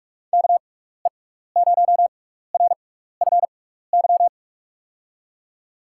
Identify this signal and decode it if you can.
Morse code, heard as a keyed tone, about 46 wpm, spaced as for 16 wpm, 700 Hz. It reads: KE0RFY
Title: USA Call Signs - 6 Characters - 6x Character Spacing - 45wpm